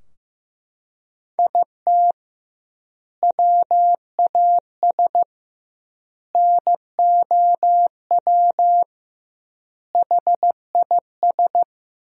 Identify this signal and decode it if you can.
Morse code, heard as a keyed tone, about 15 wpm, 700 Hz. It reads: IT WAS NOW HIS
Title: 2x Word-Spacing - Sentences from Top 200 Words - 15wpm